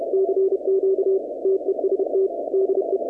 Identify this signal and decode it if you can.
Morse code, heard as a keyed tone, about 30 wpm, 380 Hz, with noise. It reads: CQ TEST DE
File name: contests / 2019-cw-wpx-kt5j-k5pi- -2330 0-21